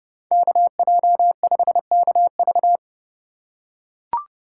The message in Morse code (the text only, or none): KJ5KV